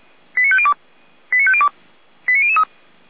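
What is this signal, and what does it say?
Five-tone selective call (ZVEI), heard as quick runs of five tones, tones of about 80 ms. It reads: 89502 89502 89003
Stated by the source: Five-tone ZVEI selective call received by SDR, from the audio server of Feuerwehr Orlamünde.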